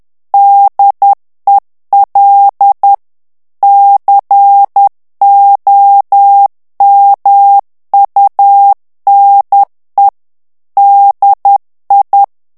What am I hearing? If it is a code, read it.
Morse code, heard as a keyed tone, about 10 wpm, 800 Hz. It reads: DEL COMUNE DI
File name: morse mp3 / ITALIEN texte clair 1 vitesse 10 mots minute